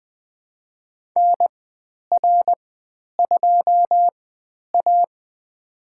Morse code, heard as a keyed tone, about 20 wpm, spaced as for 10 wpm, 700 Hz. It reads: NR2A
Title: 109 Lesson 18g - Letter m - Call Signs 20wpm